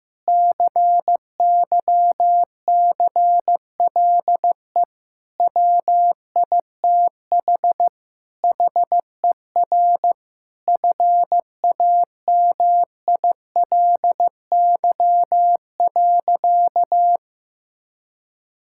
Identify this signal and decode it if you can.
Morse code, heard as a keyed tone, about 15 wpm, 700 Hz. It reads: CYCLE WITH HER FAMILY.
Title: Sentences from Top 1000 Words - Review Part 6 - 15wpm